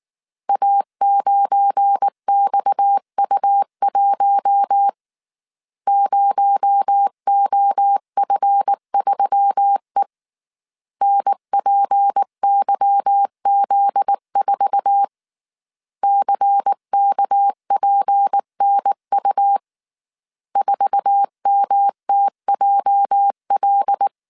Morse code, heard as a keyed tone, about 19 wpm, 800 Hz. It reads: A9XU1 0OF3E NPYZ4 CKPNU 4MTJL
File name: CW / DET / 1 9 1500C S A 18wpm 2459